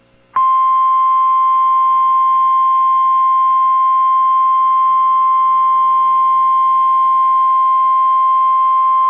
Emergency Alert System (SAME) tones (the 1050 Hz attention tone). It is heard from 0.6 to 8.0 s.